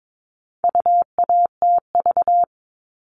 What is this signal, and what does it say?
Morse code, heard as a keyed tone, about 22 wpm, 700 Hz. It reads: UATV
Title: All Cut Numbers - 4-Digit 22wpm